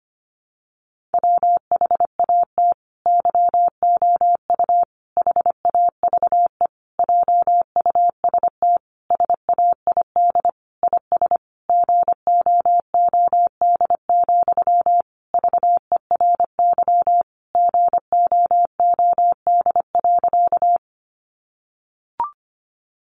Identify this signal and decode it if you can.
Morse code, heard as a keyed tone, about 25 wpm, 700 Hz. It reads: WHAT YOU HAVE JUST SAID IS GOOD, VERY GOOD.
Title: Sentences from Top 200 Words 25wpm